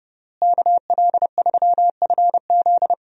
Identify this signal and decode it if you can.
Morse code, heard as a keyed tone, about 30 wpm, 700 Hz. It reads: KL3FZ
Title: USA Call Signs - 5 Characters - Repeated 3x - 30wpm